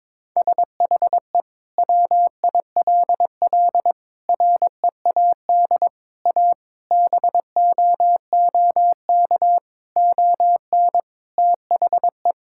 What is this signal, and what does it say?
Morse code, heard as a keyed tone, about 22 wpm, 700 Hz. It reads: SHE WILL READ A BOOK ON THE
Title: Sentences from Top 1000 Words - Review Part 6 - 22wpm